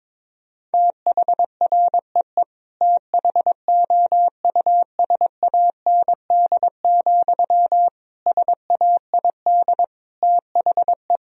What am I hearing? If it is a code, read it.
Morse code, heard as a keyed tone, about 22 wpm, 700 Hz. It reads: THREE THOUSAND, SAID THE
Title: Sentences from Top 1000 Words - Review Part 3 - 22wpm